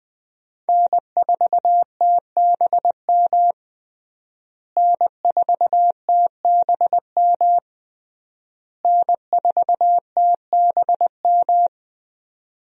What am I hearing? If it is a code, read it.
Morse code, heard as a keyed tone, about 20 wpm, 700 Hz. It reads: N4TBM N4TBM N4TBM
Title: USA Call Signs - 5 Characters - Repeated 3x - 20wpm